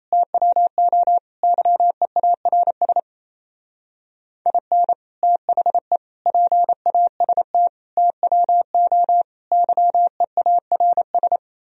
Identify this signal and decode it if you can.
Morse code, heard as a keyed tone, about 35 wpm, 700 Hz. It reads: TWO YEARS IN THE PAST TWO YEARS